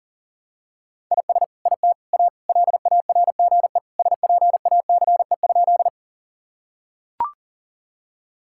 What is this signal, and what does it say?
Morse code, heard as a keyed tone, about 40 wpm, 700 Hz. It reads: IS IT A LARGE SPACE?
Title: Sentences from Top 1000 Words - Review Part 5 - 40wpm